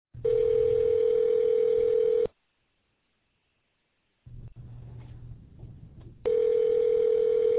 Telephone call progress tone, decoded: ringback tone

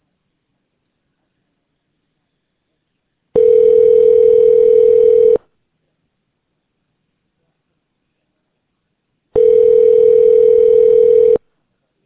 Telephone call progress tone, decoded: ringback tone

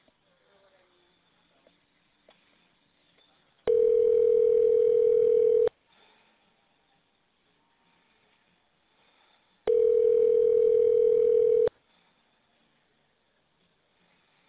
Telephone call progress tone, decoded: ringback tone

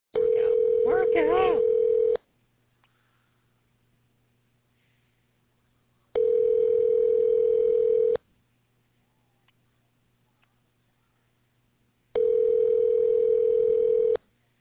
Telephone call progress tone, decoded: ringback tone

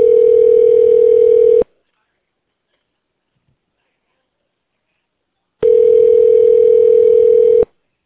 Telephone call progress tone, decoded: ringback tone

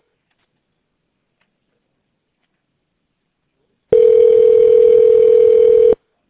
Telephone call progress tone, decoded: ringback tone

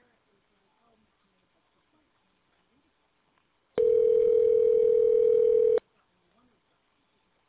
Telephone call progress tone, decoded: ringback tone

